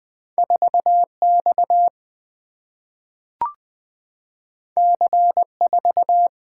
Morse code, heard as a keyed tone, about 20 wpm, 690 Hz. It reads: 4X C4
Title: Two Letter-Number Combinations - 20wpm